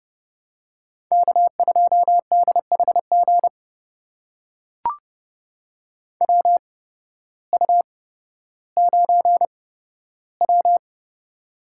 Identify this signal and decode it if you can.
Morse code, heard as a keyed tone, about 30 wpm, spaced as for 14 wpm, 700 Hz. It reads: K2DHG WU9W
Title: USA Call Signs - 8x Character Spacing - 30wpm